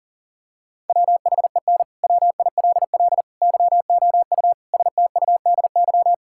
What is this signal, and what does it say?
Morse code, heard as a keyed tone, about 40 wpm, 700 Hz. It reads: WHEN WILL YOU STUDY
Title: Sentences from Top 1000 Words - Review Part 6 - 40wpm